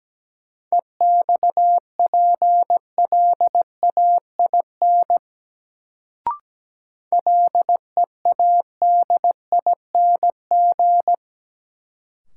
Morse code, heard as a keyed tone, about 17 wpm, 700 Hz. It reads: EXPLAIN LEADING
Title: Sentences from Top 1100 Words - New Word Drill - 17wpm